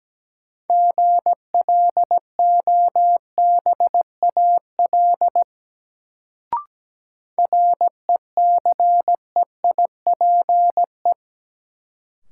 Morse code, heard as a keyed tone, about 17 wpm, 700 Hz. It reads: GLOBAL RECEIPE
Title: Sentences from Top 1200 Words - New Word Drill Alt - 17wpm